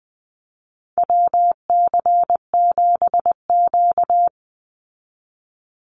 Morse code, heard as a keyed tone, about 19 wpm, 700 Hz. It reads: WC7Q
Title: Call Signs Repeated 2x - 20wpm